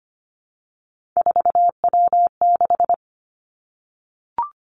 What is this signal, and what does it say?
Morse code, heard as a keyed tone, about 25 wpm, 700 Hz. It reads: 4W6
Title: Three Challenging Letter-Number Combinations 25wpm